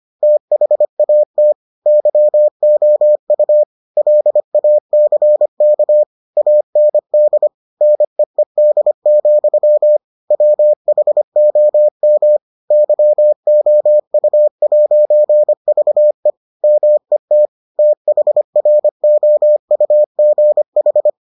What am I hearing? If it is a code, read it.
Morse code, heard as a keyed tone, about 25 wpm, 600 Hz. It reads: THAT YOU LACK AND NEED, WHOM YOU'VE MET THROUGH